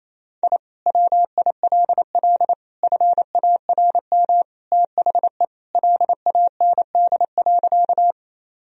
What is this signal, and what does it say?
Morse code, heard as a keyed tone, about 28 wpm, 700 Hz. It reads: I WILL FARM THE LAND.
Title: Sentences from Top 1000 Words - Review Part 2 - 28wpm